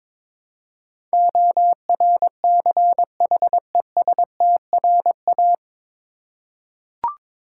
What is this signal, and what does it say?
Morse code, heard as a keyed tone, about 22 wpm, 700 Hz. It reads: ORCHESTRA
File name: 9 Letter Words - First 3 Letters Given - 22wpm